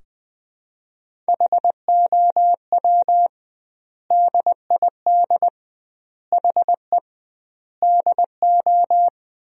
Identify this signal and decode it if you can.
Morse code, heard as a keyed tone, about 21 wpm, 700 Hz. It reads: HOW DID HE DO
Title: 2x Word-Spacing - Sentences from Top 100 Words - 20wpm